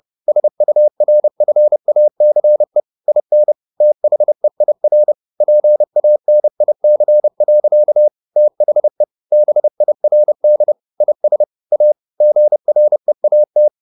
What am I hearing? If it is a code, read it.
Morse code, heard as a keyed tone, about 30 wpm, 600 Hz. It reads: SURFACE IN THEIR PANIC. THE BIRD IS A GREAT
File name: the old man and the sea / the old man and the sea 1 30wpm